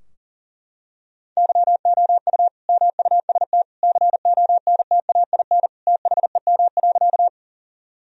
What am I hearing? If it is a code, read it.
Morse code, heard as a keyed tone, about 40 wpm, 700 Hz. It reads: YOU MUST CONTAIN THEM.
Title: Sentences from Top 1000 Words - Review Part 2 - 40wpm